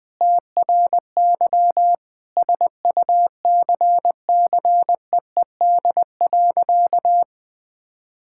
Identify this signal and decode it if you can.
Morse code, heard as a keyed tone, about 20 wpm, 700 Hz. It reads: TRY SUCCEED.
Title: Sentences with 2-Word Phrases - 20wpm